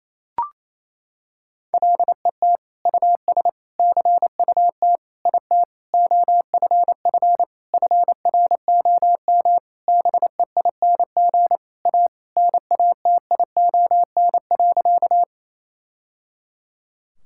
Morse code, heard as a keyed tone, about 28 wpm, 700 Hz. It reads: LET US CUT IT OFF FROM BEING A NATION.
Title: Sentences from Top 1000 Words - Review Part 6 - 28wpm